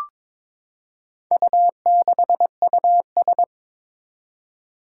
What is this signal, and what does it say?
Morse code, heard as a keyed tone, about 22 wpm, 700 Hz. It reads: U6US
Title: Four Challenging Letter-Number Combinations 22wpm